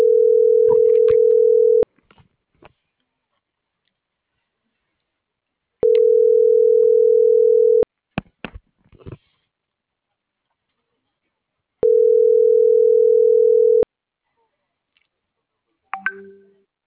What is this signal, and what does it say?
Ringback tone (North America), 440 + 480 Hz.